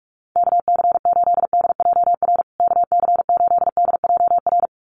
Morse code, heard as a keyed tone, about 44 wpm, 700 Hz. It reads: KC8DJR KC8DJR
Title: USA Call Signs Repeated 2x 45wpm